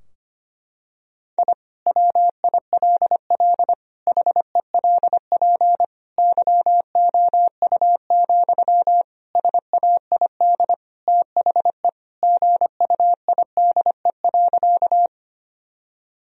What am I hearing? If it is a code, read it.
Morse code, heard as a keyed tone, about 25 wpm, 700 Hz. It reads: I WILL HELP YOU, SAID THE GUIDE.
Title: Sentences from Top 1000 Words - Review Part 3 - 25wpm